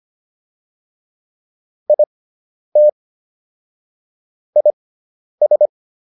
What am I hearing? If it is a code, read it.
Morse code, heard as a keyed tone, about 25 wpm, spaced as for 10 wpm, 600 Hz. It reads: IT IS